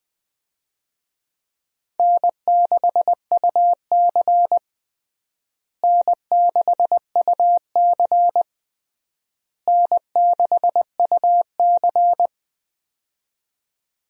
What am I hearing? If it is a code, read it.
Morse code, heard as a keyed tone, about 20 wpm, 700 Hz. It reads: N6UC N6UC N6UC
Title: USA Call Signs - 4 Characters - Repeated 3x - 20wpm